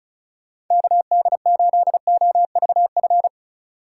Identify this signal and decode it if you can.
Morse code, heard as a keyed tone, about 33 wpm, 700 Hz. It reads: KD8OVF
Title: Speed Racing - USA 2x3 Call Signs 35wpm